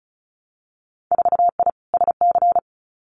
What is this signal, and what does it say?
Morse code, heard as a keyed tone, about 36 wpm, 700 Hz. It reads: 4I SC
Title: Winter Field Day Exchanges 35wpm